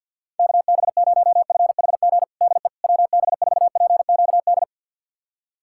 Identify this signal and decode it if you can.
Morse code, heard as a keyed tone, about 51 wpm, 700 Hz. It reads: KB0FHG DE WB4PQD
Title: USA Callsign DE Callsign - 6 Characters - 50wpm